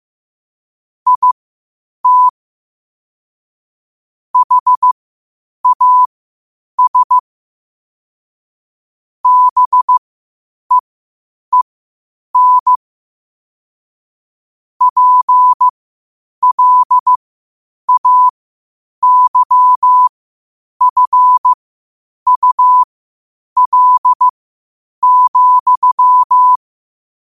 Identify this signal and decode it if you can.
Morse code, heard as a keyed tone, about 15 wpm, spaced as for 8 wpm, 1000 Hz. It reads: IT HAS BEEN PLAYFUL,
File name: qotdfiles / 8wpm